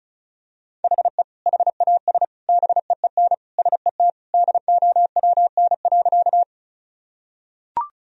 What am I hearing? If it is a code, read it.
Morse code, heard as a keyed tone, about 36 wpm, 700 Hz. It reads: HE HAS BEEN SET DOWN.
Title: Sentences from Top 1000 Words - Review Part 1 - 35wpm